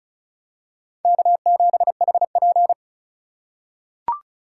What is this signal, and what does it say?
Morse code, heard as a keyed tone, about 36 wpm, 700 Hz. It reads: K7HP
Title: USA Call Signs - 4 Characters - 3x Character Spacing - 35wpm